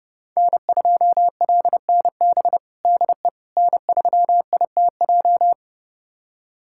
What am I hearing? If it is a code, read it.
Morse code, heard as a keyed tone, about 30 wpm, 700 Hz. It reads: N2LNB DE N3ITJ
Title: USA Callsign DE Callsign - 5 Characters - 30wpm